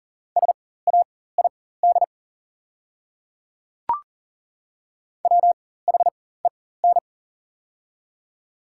Morse code, heard as a keyed tone, about 40 wpm, spaced as for 19 wpm, 690 Hz. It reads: SAID WHEN
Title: 4x Character Spacing - Top 300 Words - 40wpm